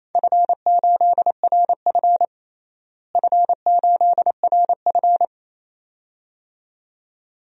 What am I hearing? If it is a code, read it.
Morse code, heard as a keyed tone, about 28 wpm, 700 Hz. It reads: F8RF F8RF
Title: DXCC-100 Call Signs - 4 Characters - Repeated 2x - 28wpm